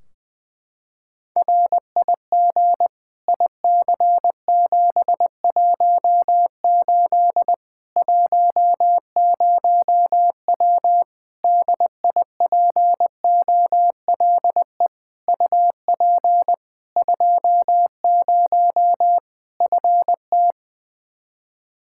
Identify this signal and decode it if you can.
Morse code, heard as a keyed tone, about 20 wpm, 700 Hz. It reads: RIG IC718 10W DIPOLE UP 20 FT
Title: Practice QSO - Exchange 2 - Info - 20wpm